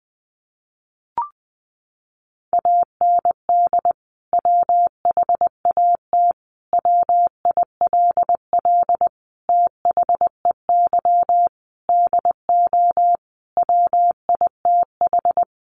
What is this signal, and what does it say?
Morse code, heard as a keyed tone, about 20 wpm, 700 Hz. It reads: AND WHAT WILL THEY DO WITH
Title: Sentences from Top 200 Words 20wpm